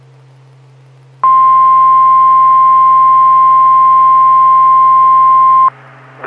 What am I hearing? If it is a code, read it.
Emergency Alert System (SAME) tones (the 1050 Hz attention tone). It is heard from 1.2 to 5.7 s.
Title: Saline, AK SKYWARN Repeater System - W5RHS